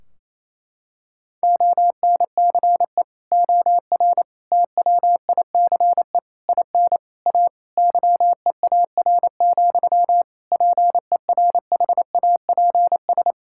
Morse code, heard as a keyed tone, about 28 wpm, 700 Hz. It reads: ONCE OR TWICE IN A YEAR, PERHAPS
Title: Sentences from Top 1600 Words 28wpm